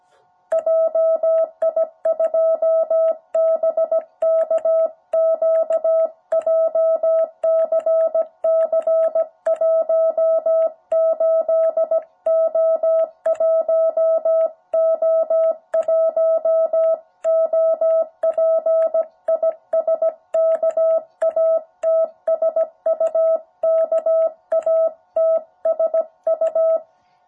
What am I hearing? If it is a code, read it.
Morse code, heard as a keyed tone, about 17 wpm, 660 Hz. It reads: JI2BKQ JCC 18O1O1 OP IS KATSU KATSU
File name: jh8cbh / 2021cw